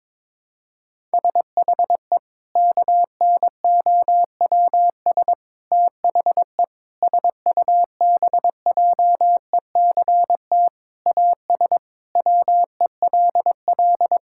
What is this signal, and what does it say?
Morse code, heard as a keyed tone, about 22 wpm, 700 Hz. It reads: SHE KNOWS THE SUBJECT AS WELL
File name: practice / 22wpm / Sentences with 4-Word Phrases - 22wpm